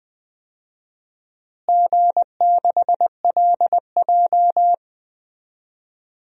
Morse code, heard as a keyed tone, about 20 wpm, 700 Hz. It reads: G6LJ
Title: DXCC-100 Call Signs - 4 Characters - Repeated 3x - 20wpm